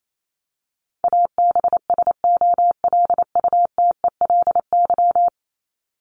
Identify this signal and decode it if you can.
Morse code, heard as a keyed tone, about 28 wpm, 700 Hz. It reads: ABSOLUTELY
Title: LY Suffix - Top 100 Words with Base Words 28wpm